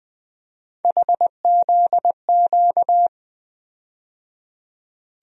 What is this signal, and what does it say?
Morse code, heard as a keyed tone, about 20 wpm, 700 Hz. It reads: HZQ
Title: USA Call Sign Postfixes - 3 Letters 20wpm